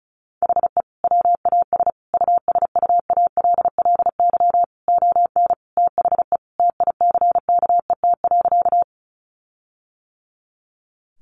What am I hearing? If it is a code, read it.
Morse code, heard as a keyed tone, about 34 wpm, 700 Hz. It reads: HE WAS USUALLY ON THE TICKET.
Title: Sentences from Top 1400 Words 35wpm